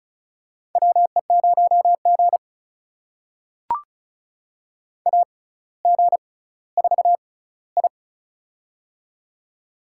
Morse code, heard as a keyed tone, about 36 wpm, spaced as for 18 wpm, 700 Hz. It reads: WE0G AG4I
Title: USA Call Signs - 4 Characters - 6x Character Spacing - 35wpm